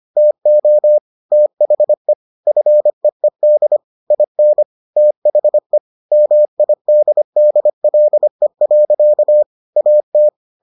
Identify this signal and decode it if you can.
Morse code, heard as a keyed tone, about 25 wpm, 600 Hz. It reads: TO THE FEED IN THE MIDDLE. AT